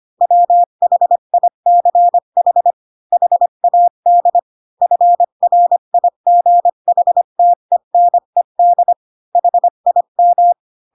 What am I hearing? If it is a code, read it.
Morse code, heard as a keyed tone, about 25 wpm, 700 Hz. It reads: WHICH HAD FRIGHTENED HIM